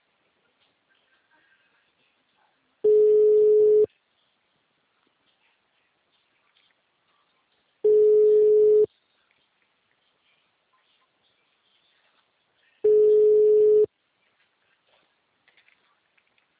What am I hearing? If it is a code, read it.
Ringback tone (Europe, 425 Hz).
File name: out- -2004- - - .12450